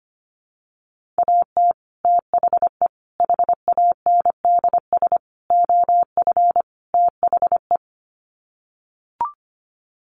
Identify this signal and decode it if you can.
Morse code, heard as a keyed tone, about 25 wpm, 700 Hz. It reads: AT THE HANDS OF THE
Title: Sets of 5 Words Encore2 25wpm